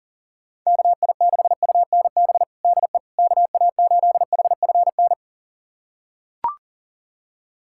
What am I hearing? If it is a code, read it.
Morse code, heard as a keyed tone, about 40 wpm, 700 Hz. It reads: KI6UNB DE KA8HFN
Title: USA Callsign DE Callsign - 6 Characters - 40wpm